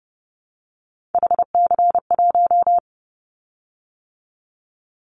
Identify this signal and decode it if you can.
Morse code, heard as a keyed tone, about 30 wpm, 700 Hz. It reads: HC1
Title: Callsign Prefixes for Top 100 DXCC - 30wpm